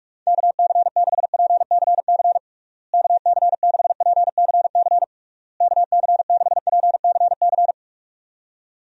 Morse code, heard as a keyed tone, about 46 wpm, 700 Hz. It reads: KC6PCC KC6PCC KC6PCC
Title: USA Call Signs - 6 Characters - Repeated 3x - 45wpm